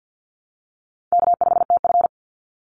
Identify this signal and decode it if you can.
Morse code, heard as a keyed tone, about 50 wpm, 700 Hz. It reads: K5TF
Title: Easy Call Signs Repeated 2x - 50wpm